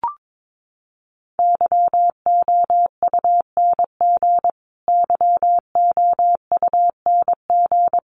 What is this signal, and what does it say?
Morse code, heard as a keyed tone, about 22 wpm, 700 Hz. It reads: YOUNG YOUNG
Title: Top 500 Words Repeated 22wpm